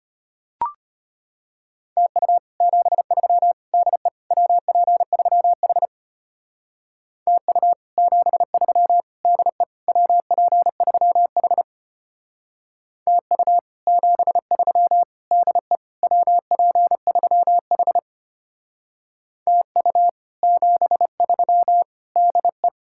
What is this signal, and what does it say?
Morse code, heard as a keyed tone, about 32 wpm, 700 Hz. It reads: TU 73 DE WP3H TU 73 DE WP3H TU 73 DE WP3H TU 73 DE